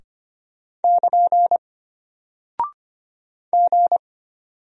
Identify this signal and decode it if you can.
Morse code, heard as a keyed tone, about 25 wpm, 700 Hz. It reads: <KN> G
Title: Single Character (Letter:Number:Punctuation:Prosign) Alt 25wpm